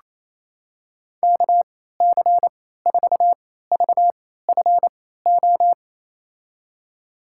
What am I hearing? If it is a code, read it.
Morse code, heard as a keyed tone, about 28 wpm, spaced as for 16 wpm, 700 Hz. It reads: KC4VFO
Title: USA Call Signs - 6 Characters - 3x Character Spacing - 28wpm